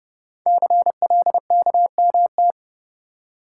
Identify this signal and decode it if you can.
Morse code, heard as a keyed tone, about 30 wpm, 700 Hz. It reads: CLKMT
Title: Code Groups - 5 Letters - 30wpm